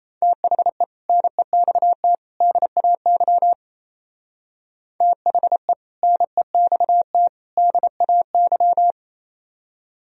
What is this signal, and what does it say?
Morse code, heard as a keyed tone, about 35 wpm, 700 Hz. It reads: THE NEXT DAY THE NEXT DAY
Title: Speed-Racing - Sets of 3 Words 28wpm